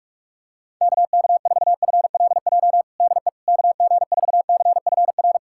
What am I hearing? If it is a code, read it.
Morse code, heard as a keyed tone, about 43 wpm, 700 Hz. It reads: KK4FLJ DE KG4CFR
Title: USA Callsign DE Callsign - 6 Characters - 45wpm